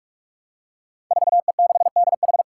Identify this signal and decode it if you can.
Morse code, heard as a keyed tone, about 46 wpm, 700 Hz. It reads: 4E6DH
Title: Five Challenging Letter-Number Combinations 45wpm